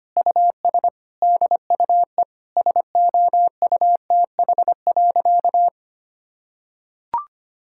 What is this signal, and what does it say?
Morse code, heard as a keyed tone, about 25 wpm, 700 Hz. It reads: US DUE SOUTH.
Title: Sentences from Top 1000 Words - Review Part 6 - 25wpm